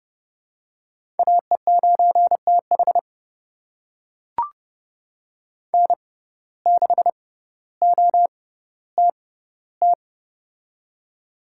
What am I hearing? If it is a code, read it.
Morse code, heard as a keyed tone, about 29 wpm, spaced as for 15 wpm, 700 Hz. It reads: AE9TH N6OTT
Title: USA Call Signs - 5 Characters - 6x Character Spacing - 30wpm